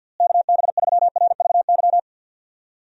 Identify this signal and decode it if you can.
Morse code, heard as a keyed tone, about 52 wpm, 700 Hz. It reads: KB3RVY
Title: USA Call Signs - 6 Characters - Repeated 3x - 50wpm